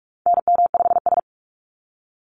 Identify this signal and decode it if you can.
Morse code, heard as a keyed tone, about 44 wpm, 700 Hz. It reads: NM5S
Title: NA SOTA Activator Callsigns 45wpm